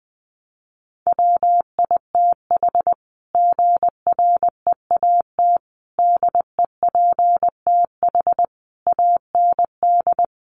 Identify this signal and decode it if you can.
Morse code, heard as a keyed tone, about 19 wpm, 700 Hz. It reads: WITH GREAT DEPTH AND
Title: Sentences from Top 1200 Words 20wpm